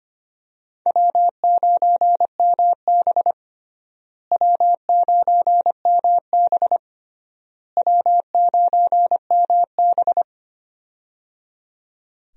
Morse code, heard as a keyed tone, about 25 wpm, 700 Hz. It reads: W9MB W9MB W9MB
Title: USA Call Signs - 4 Characters - Repeated 3x - 25wpm